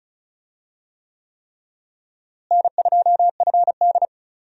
Morse code, heard as a keyed tone, about 34 wpm, 700 Hz. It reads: N2FD
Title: USA Call Signs - 4 Characters - Repeated 3x - 35wpm